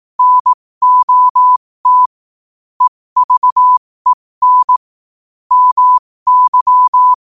Morse code, heard as a keyed tone, about 18 wpm, spaced as for 15 wpm, 1000 Hz. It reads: NOT EVEN MY
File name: qotdfiles / 15wpm